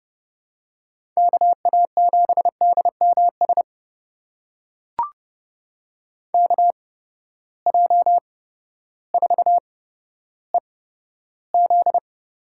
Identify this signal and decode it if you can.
Morse code, heard as a keyed tone, about 30 wpm, spaced as for 14 wpm, 700 Hz. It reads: KA7DMS KJ4EZ
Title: USA Call Signs - 6 Characters - 8x Character Spacing - 30wpm